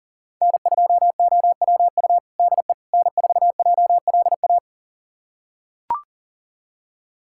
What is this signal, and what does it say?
Morse code, heard as a keyed tone, about 40 wpm, 700 Hz. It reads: N2OWU DE N4JLA
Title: USA Callsign DE Callsign - 5 Characters - 40wpm